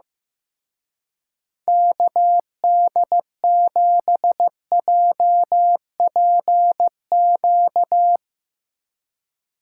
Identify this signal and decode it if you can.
Morse code, heard as a keyed tone, about 15 wpm, 700 Hz. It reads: KD7JPQ
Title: Speed Racing - USA 2x3 Call Signs 15wpm